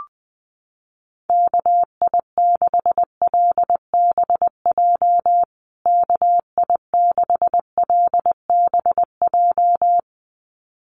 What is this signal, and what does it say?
Morse code, heard as a keyed tone, about 20 wpm, 700 Hz. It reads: KI6LBJ KI6LBJ
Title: USA Call Signs Repeated 2x 20wpm